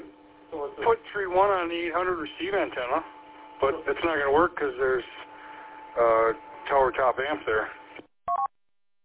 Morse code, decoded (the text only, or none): none